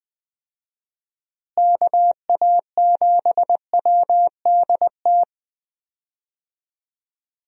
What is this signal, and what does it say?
Morse code, heard as a keyed tone, about 20 wpm, 700 Hz. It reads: KA7WDT
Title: USA Call Signs - 6 Characters - Repeated 3x - 20wpm